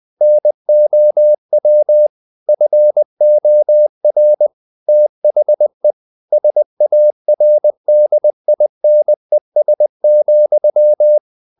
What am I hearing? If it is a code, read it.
Morse code, heard as a keyed tone, about 20 wpm, 600 Hz. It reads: NOW FOR THE SARDINES,
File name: the old man and the sea / the old man and the sea part 1 20wpm